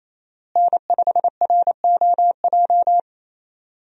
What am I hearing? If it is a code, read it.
Morse code, heard as a keyed tone, about 28 wpm, 700 Hz. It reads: N5ROJ
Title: USA Call Signs - 5 Characters - Repeated 2x - 28wpm